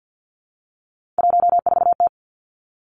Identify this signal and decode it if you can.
Morse code, heard as a keyed tone, about 50 wpm, 700 Hz. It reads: 14T
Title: Three Challenging Letter-Number Combinations 50wpm